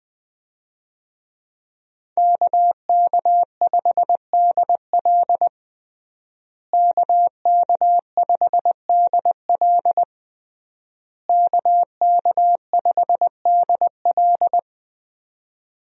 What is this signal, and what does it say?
Morse code, heard as a keyed tone, about 20 wpm, 700 Hz. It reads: KK5DL KK5DL KK5DL
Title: USA Call Signs - 5 Characters - Repeated 3x - 20wpm